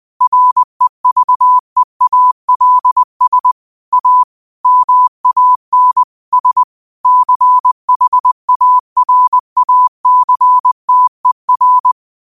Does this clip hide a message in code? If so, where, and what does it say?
Yes: Morse code from 0.2 to 11.9 s, heard as a keyed tone, about 20 wpm, 1000 Hz. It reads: REVEALS A MAN S CHARACTER